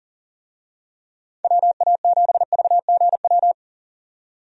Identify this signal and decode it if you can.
Morse code, heard as a keyed tone, about 40 wpm, 700 Hz. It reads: WA7VGW
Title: USA Call Signs - 6 Characters - 40wpm